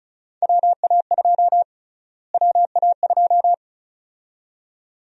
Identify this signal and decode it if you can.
Morse code, heard as a keyed tone, about 34 wpm, 700 Hz. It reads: WA2 WA2
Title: USA Call Sign Prefixes - Repeated 2x - 35wpm